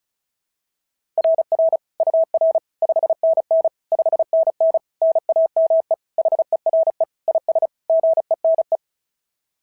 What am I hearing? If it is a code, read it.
Morse code, heard as a keyed tone, about 36 wpm, 690 Hz. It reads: RR UR 5NN 5NN NAME HERE IS GENE
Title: SOTA QSO Practice 35wpm